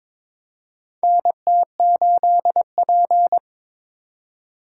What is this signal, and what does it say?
Morse code, heard as a keyed tone, about 22 wpm, 700 Hz. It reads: NT8P
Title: USA Call Signs - 4x Character Spacing - 22wpm